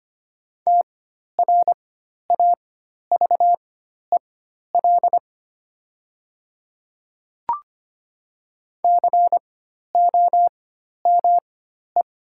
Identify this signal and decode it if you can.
Morse code, heard as a keyed tone, about 25 wpm, spaced as for 12 wpm, 690 Hz. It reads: TRAVEL COME
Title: 4x Character Spacing - Top 500 Words - 25wpm